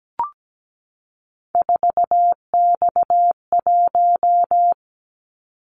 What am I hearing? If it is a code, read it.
Morse code, heard as a keyed tone, about 17 wpm, 700 Hz. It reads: 4X1
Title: Callsign Prefixes for Top 50 DXCC - 17wpm